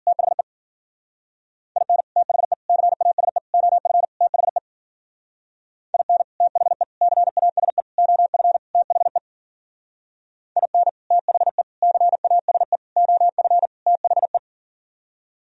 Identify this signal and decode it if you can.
Morse code, heard as a keyed tone, about 49 wpm, 700 Hz. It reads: THE IN THE CASE OF THE IN THE CASE OF THE IN THE CASE OF THE
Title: Speed-Racing - Sets of 5 Words 40wpm